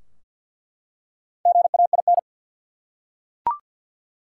Morse code, heard as a keyed tone, about 48 wpm, 700 Hz. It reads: GAIN
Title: Four Letter Words - 50wpm